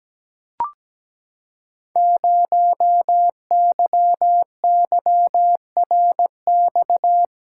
Morse code, heard as a keyed tone, about 17 wpm, 700 Hz. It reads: 0YYRX
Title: Code Groups - 5 Letter:Numbers - 17wpm